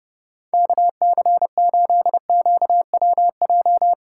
Morse code, heard as a keyed tone, about 32 wpm, 700 Hz. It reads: KC8QWJ